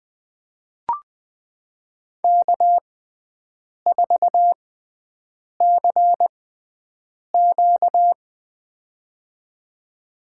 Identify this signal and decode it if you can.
Morse code, heard as a keyed tone, about 20 wpm, spaced as for 7 wpm, 700 Hz. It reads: K4CQ
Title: USA Call Signs - 4 Characters - 6x Character Spacing - 20wpm